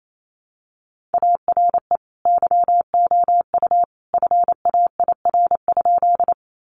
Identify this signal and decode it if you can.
Morse code, heard as a keyed tone, about 28 wpm, 700 Hz. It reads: ARE YOU FAIR?
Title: Sentences from Top 1300 Words 28wpm